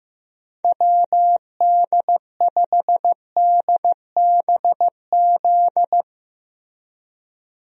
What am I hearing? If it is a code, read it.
Morse code, heard as a keyed tone, about 15 wpm, 700 Hz. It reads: WD5DBZ
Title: USA Call Signs - 6 Characters - Repeated 2x - 15wpm